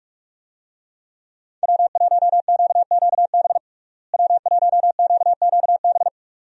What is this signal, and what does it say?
Morse code, heard as a keyed tone, about 42 wpm, 700 Hz. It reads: W1QQB W1QQB
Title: USA Call Signs - 5 Characters - Repeated 2x - 45wpm